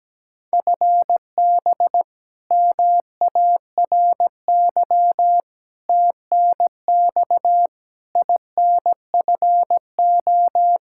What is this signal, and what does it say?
Morse code, heard as a keyed tone, about 17 wpm, 700 Hz. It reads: FB MARY TNX INFO
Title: Practice QSO - Exchange 3 - More Info - 17wpm